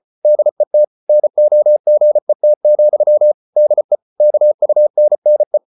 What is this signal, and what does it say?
Morse code, heard as a keyed tone, about 34 wpm, 600 Hz. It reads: DET NOGET, DE KUNNE
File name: Ferie QRQ 34wpm